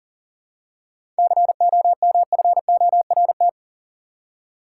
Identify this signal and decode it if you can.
Morse code, heard as a keyed tone, about 40 wpm, 700 Hz. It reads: COMFORT
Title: Seven Letter Words - 40wpm